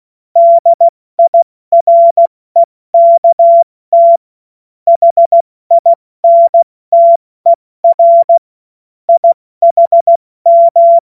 Morse code, heard as a keyed tone, about 16 wpm, spaced as for 14 wpm, 680 Hz. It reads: DIREKT HINTER IHM
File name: Morsekurs / Witze / Witz 19 Dezember 2023 70 F80